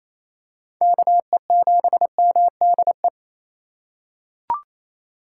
Morse code, heard as a keyed tone, about 28 wpm, 700 Hz. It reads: KE7MDE
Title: USA Call Signs - 6 Characters - 3x Character Spacing - 28wpm